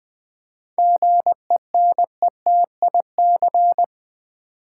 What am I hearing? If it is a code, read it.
Morse code, heard as a keyed tone, about 20 wpm, 700 Hz. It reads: GENETIC
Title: 7 Letter Words - First 3 Letters Given - 20wpm